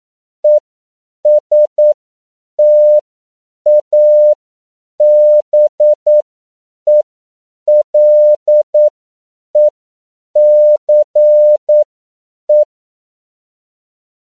Morse code, heard as a keyed tone, about 9 wpm, spaced as for 7 wpm, 600 Hz. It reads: ESTABELECE
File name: CW / 10PPM